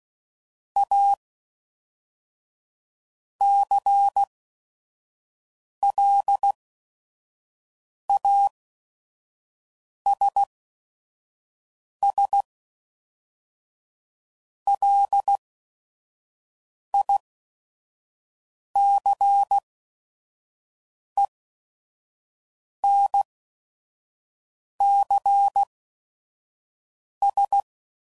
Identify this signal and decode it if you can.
Morse code, heard as a keyed tone, about 16 wpm, spaced as for 5 wpm, 800 Hz. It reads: A CLASS LICENCS
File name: CW Code Groups @5WPM / QSOs / kc7sac